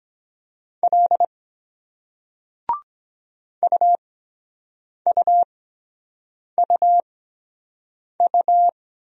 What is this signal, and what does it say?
Morse code, heard as a keyed tone, about 23 wpm, spaced as for 7 wpm, 700 Hz. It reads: L UUUU